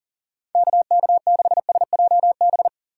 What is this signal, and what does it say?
Morse code, heard as a keyed tone, about 53 wpm, 700 Hz. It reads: KK6SJB